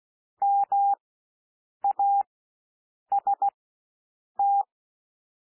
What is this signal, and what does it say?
Morse code, heard as a keyed tone, about 16 wpm, spaced as for 7 wpm, 800 Hz. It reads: MAST
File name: cwmp3 / W1 SOUND WORDS 16 wpm